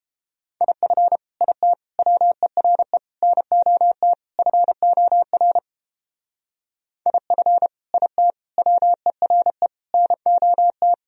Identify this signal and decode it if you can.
Morse code, heard as a keyed tone, about 30 wpm, 700 Hz. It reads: IF IT WERE NOT FOR IF IT WERE NOT